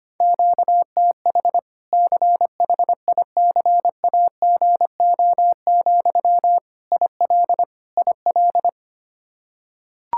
Morse code, heard as a keyed tone, about 25 wpm, 700 Hz. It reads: QTH CHICAGO, IL IL
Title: Practice QSO - Exchange 1 - Beginning - 25wpm